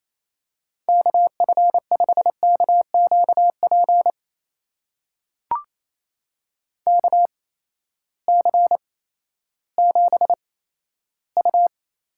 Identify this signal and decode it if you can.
Morse code, heard as a keyed tone, about 28 wpm, spaced as for 14 wpm, 700 Hz. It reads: KF5KQP KC7U
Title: USA Call Signs - 6 Characters - 8x Character Spacing - 28wpm